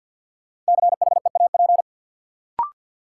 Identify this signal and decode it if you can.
Morse code, heard as a keyed tone, about 52 wpm, 700 Hz. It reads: CHEAP
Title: Sentences from Top 1200 Words - New Word Drill - 50wpm